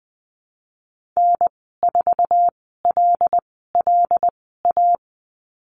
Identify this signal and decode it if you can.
Morse code, heard as a keyed tone, about 20 wpm, spaced as for 14 wpm, 700 Hz. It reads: N4LLA
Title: 076 Lesson 13g - Letter L - Call Signs 20wpm